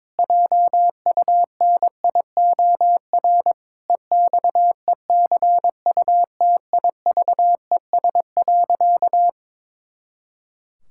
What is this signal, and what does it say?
Morse code, heard as a keyed tone, about 22 wpm, 700 Hz. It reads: JUNIOR EXECUTIVES.
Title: Pangram Challenge v2 - 22wpm